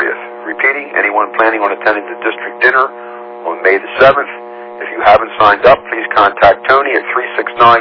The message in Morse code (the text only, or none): none